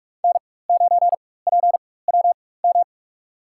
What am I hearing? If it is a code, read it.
Morse code, heard as a keyed tone, about 46 wpm, spaced as for 21 wpm, 700 Hz. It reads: N9PWM
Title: USA Call Signs - 5 Characters - 4x Character Spacing - 45wpm